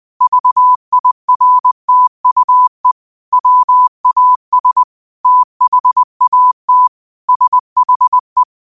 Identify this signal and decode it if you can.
Morse code, heard as a keyed tone, about 20 wpm, 1000 Hz. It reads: VIRTUE WAS THAT SHE
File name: qotdfiles / 20wpm